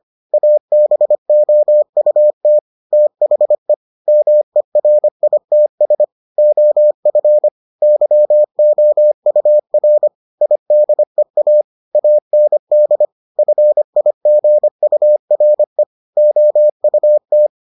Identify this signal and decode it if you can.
Morse code, heard as a keyed tone, about 25 wpm, 600 Hz. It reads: ABOUT THE MERITS OF YOUR IDEA AND FIGURE OUT